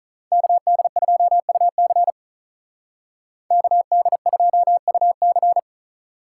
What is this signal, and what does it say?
Morse code, heard as a keyed tone, about 43 wpm, 700 Hz. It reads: KD2UC KD2UC
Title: Speed Racing - USA Call Signs 35wpm